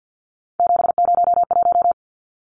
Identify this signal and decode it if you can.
Morse code, heard as a keyed tone, about 50 wpm, 700 Hz. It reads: 701
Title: Numbers - 3-Digit 50wpm